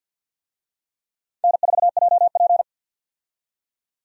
Morse code, heard as a keyed tone, about 46 wpm, 700 Hz. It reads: N4JP
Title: USA Call Signs - 4 Characters - 50wpm